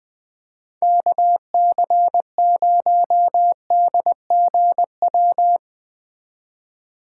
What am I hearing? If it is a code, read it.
Morse code, heard as a keyed tone, about 19 wpm, 700 Hz. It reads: KC0DGW
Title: Speed Racing - USA 2x3 Call Signs 17wpm